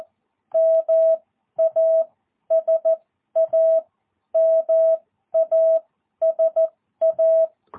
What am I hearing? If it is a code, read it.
Morse code, heard as a keyed tone, about 14 wpm, spaced as for 11 wpm, 650 Hz. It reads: MASA MASA
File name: jh8cbh / 2022cw